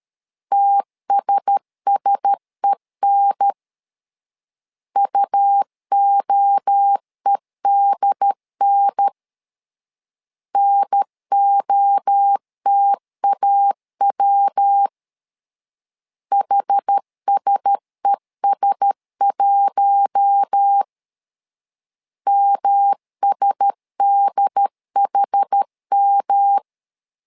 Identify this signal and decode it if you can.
Morse code, heard as a keyed tone, about 13 wpm, 800 Hz. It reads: TSSEN UOEDN NOTAW HSES1 MSDHM
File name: LEZIONI / 4 Q1S 12wpm 1802